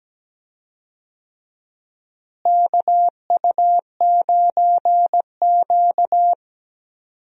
Morse code, heard as a keyed tone, about 17 wpm, 700 Hz. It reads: KU9Q
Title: USA Call Signs - 4 Characters - Repeated 2x - 17wpm